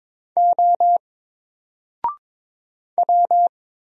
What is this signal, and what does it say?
Morse code, heard as a keyed tone, about 22 wpm, 700 Hz. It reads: O W
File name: Single-Letter-Number-22wpm